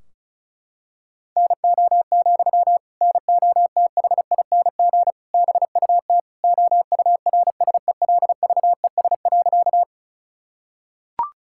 Morse code, heard as a keyed tone, about 36 wpm, 700 Hz. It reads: NO, NOTHING BUT OURSELVES.
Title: Sentences from Top 1600 Words 35wpm